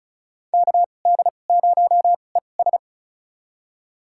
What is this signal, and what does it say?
Morse code, heard as a keyed tone, about 36 wpm, spaced as for 25 wpm, 700 Hz. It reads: KD0ES
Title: USA Call Signs - 5 Characters - 2x Character Spacing - 35wpm